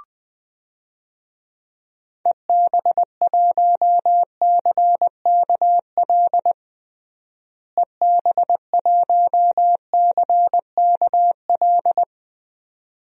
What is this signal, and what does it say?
Morse code, heard as a keyed tone, about 20 wpm, 700 Hz. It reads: EB1CKL EB1CKL
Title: DXCC-100 Call Signs - 6 Characters - Repeated 2x - 20wpm